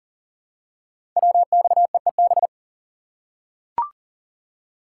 Morse code, heard as a keyed tone, about 42 wpm, 700 Hz. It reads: WXEEB
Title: Code Groups - 5 Letters - 40wpm